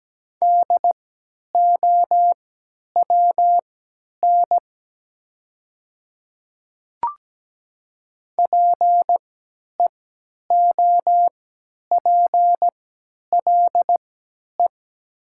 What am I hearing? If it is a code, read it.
Morse code, heard as a keyed tone, about 17 wpm, spaced as for 10 wpm, 700 Hz. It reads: DOWN PEOPLE
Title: 3x Character Spacing - Top 200 Words - 17wpm